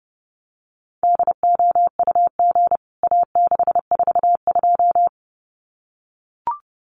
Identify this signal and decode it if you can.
Morse code, heard as a keyed tone, about 30 wpm, 700 Hz. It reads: DOUG A642
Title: CWT Exchange 30wpm